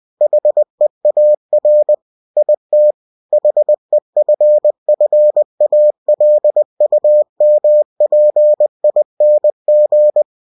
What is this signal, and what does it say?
Morse code, heard as a keyed tone, about 20 wpm, 600 Hz. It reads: HEAR IT HEFFALUMPING
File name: cw-books / winnie-the-pooh / winnie-20wpm-page-0041